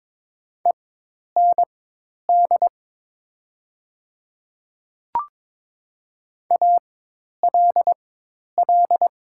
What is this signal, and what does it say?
Morse code, heard as a keyed tone, about 22 wpm, spaced as for 10 wpm, 700 Hz. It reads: END ALL
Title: 4x Character Spacing - Top 300 Words - 22wpm